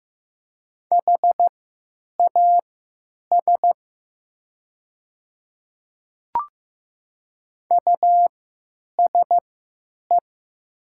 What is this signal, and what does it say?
Morse code, heard as a keyed tone, about 15 wpm, spaced as for 8 wpm, 700 Hz. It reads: HAS USE
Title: 3x Character Spacing - Top 100 Words - 15wpm